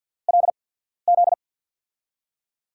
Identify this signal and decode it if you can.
Morse code, heard as a keyed tone, about 48 wpm, spaced as for 13 wpm, 700 Hz. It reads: LZ
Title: USA Call Signs - 6 Characters - 8x Character Spacing - 50wpm